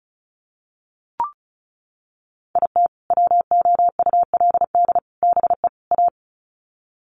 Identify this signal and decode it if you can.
Morse code, heard as a keyed tone, about 34 wpm, 700 Hz. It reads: IT WOULD BE A
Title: Sets of 4 Words 35wpm